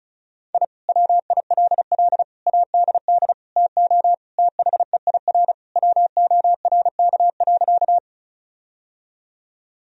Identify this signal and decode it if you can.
Morse code, heard as a keyed tone, about 36 wpm, 700 Hz. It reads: I WILL ADD TO THEIR WORK.
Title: Sentences from Top 1000 Words - Review Part 1 - 35wpm